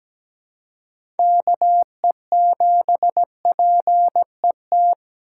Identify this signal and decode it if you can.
Morse code, heard as a keyed tone, about 17 wpm, 700 Hz. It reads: KE7PET
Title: USA Call Signs - 6x Character Spacing - 17wpm